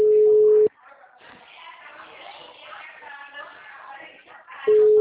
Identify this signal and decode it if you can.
Ringback tone (Europe, 425 Hz), with speech.